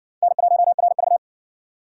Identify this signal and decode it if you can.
Morse code, heard as a keyed tone, about 60 wpm, 700 Hz. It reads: N1RV